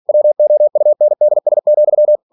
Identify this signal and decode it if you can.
Morse code, heard as a keyed tone, about 46 wpm, 600 Hz. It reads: WOUNDS,
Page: Never 47wpm